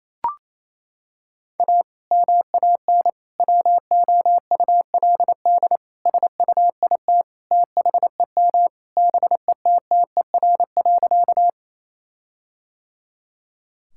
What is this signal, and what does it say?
Morse code, heard as a keyed tone, about 28 wpm, 700 Hz. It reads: A MAN WOULD SUIT THEM BETTER.
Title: Sentences from Top 1600 Words 28wpm